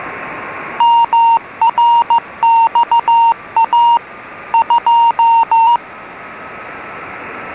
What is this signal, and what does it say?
Morse code, heard as a keyed tone, about 15 wpm, 930 Hz. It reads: MRXA 2